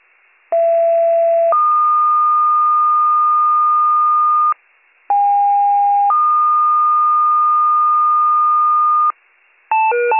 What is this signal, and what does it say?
Two-tone page, heard as a one-second tone, then a three-second tone, about 670 then 1220 Hz, about 810 then 1220 Hz. It is heard from 0.5 to 9.1 s.